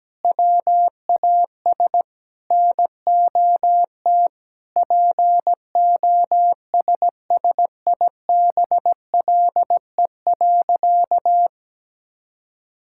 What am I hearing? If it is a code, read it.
Morse code, heard as a keyed tone, about 17 wpm, 700 Hz. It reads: WAS NOT POSSIBLE.
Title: Sentences from Top 1000 Words - Review Part 6 - 17wpm